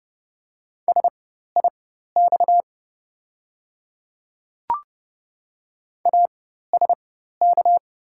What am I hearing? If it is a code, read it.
Morse code, heard as a keyed tone, about 31 wpm, spaced as for 14 wpm, 700 Hz. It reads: SIX ASK
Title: 4x Character Spacing - Top 500 Words - 30wpm